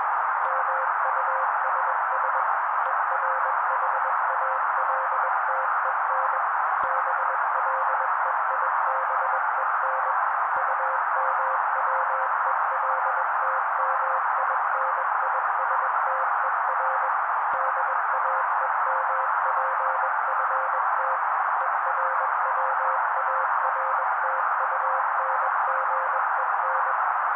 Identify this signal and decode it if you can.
Morse code, heard as a keyed tone, about 21 wpm, 530 Hz. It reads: MUSS ERHALTEN BLEIBEN UMWELTMINISTER DAEMPFT ERWARTUNGEN
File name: daily / 20wpm noise